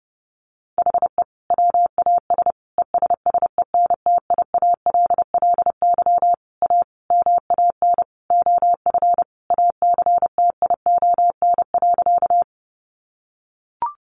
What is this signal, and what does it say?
Morse code, heard as a keyed tone, about 30 wpm, 700 Hz. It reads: HE WAS ESSENTIALLY A MAN OF ACTION.
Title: Sentences from Top 1300 Words 30wpm